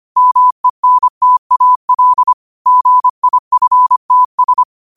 Morse code, heard as a keyed tone, about 25 wpm, 1000 Hz. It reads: MENTAL GIFTS
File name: qotdfiles / 25wpm